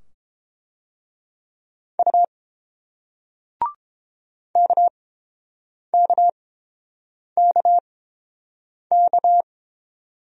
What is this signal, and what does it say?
Morse code, heard as a keyed tone, about 30 wpm, spaced as for 7 wpm, 700 Hz. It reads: U KKKK